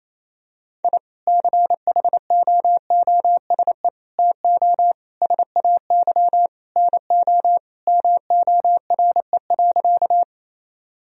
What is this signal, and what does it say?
Morse code, heard as a keyed tone, about 28 wpm, 700 Hz. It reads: I CHOOSE TO SAY NO MORE.
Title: Sentences from Top 1000 Words - Review Part 4 - 28wpm